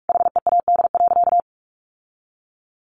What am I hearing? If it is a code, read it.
Morse code, heard as a keyed tone, about 44 wpm, 700 Hz. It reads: HEAD.
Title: Sentences from Top 1100 Words 45wpm